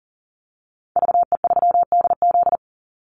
Morse code, heard as a keyed tone, about 40 wpm, 700 Hz. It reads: VE3DZ
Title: Call Signs Repeated 3x - 40wpm